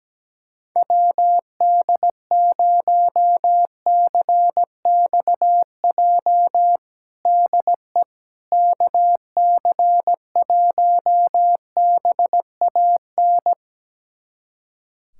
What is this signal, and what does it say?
Morse code, heard as a keyed tone, about 17 wpm, 700 Hz. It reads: WD0CXJ DE KC1BAN
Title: USA Callsign DE Callsign - 6 Characters - 17wpm